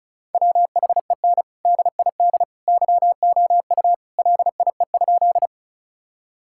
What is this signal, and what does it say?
Morse code, heard as a keyed tone, about 34 wpm, 700 Hz. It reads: WHEN DID YOU LIE?
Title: Sentences from Top 1600 Words 35wpm